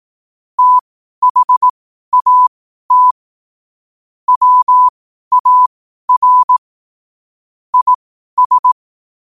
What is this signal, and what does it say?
Morse code, heard as a keyed tone, about 18 wpm, spaced as for 12 wpm, 1000 Hz. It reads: THAT WAR IS
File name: qotdfiles / 12wpm